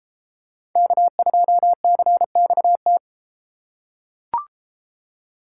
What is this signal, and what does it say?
Morse code, heard as a keyed tone, about 33 wpm, 700 Hz. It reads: K2CXT